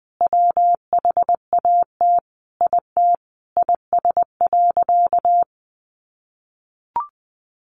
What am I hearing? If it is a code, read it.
Morse code, heard as a keyed tone, about 20 wpm, 700 Hz. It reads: WHAT IT IS.
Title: Sentences from Top 700 Words 20wpm